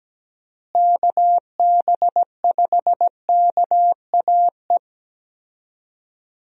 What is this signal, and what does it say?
Morse code, heard as a keyed tone, about 17 wpm, 700 Hz. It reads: KB5KAE
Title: USA Call Signs - 6 Characters - Repeated 2x - 17wpm